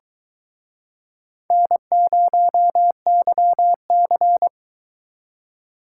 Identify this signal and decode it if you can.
Morse code, heard as a keyed tone, about 23 wpm, 700 Hz. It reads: N0YC